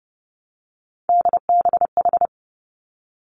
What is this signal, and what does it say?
Morse code, heard as a keyed tone, about 30 wpm, 700 Hz. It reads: DBH
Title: Three Challenging Letter-Number Combinations 30wpm